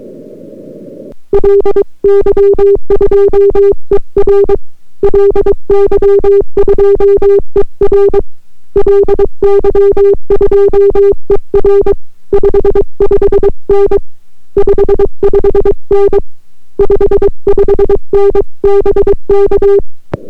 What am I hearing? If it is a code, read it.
Morse code, heard as a keyed tone, about 22 wpm, 400 Hz. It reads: LY2ER LY2ER LY2ER 55N 55N 55N BK